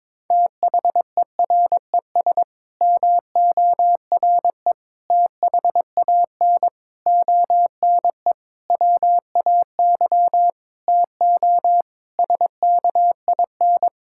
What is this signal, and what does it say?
Morse code, heard as a keyed tone, about 22 wpm, 700 Hz. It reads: THERES MORE THAN ONE WAY TO SKIN
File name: Sets of 9 Words - 22wpm